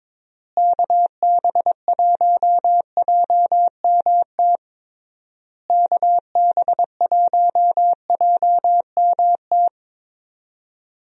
Morse code, heard as a keyed tone, about 22 wpm, 700 Hz. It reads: KB1JMT KB1JMT
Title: USA Call Signs - 6 Characters - Repeated 3x - 22wpm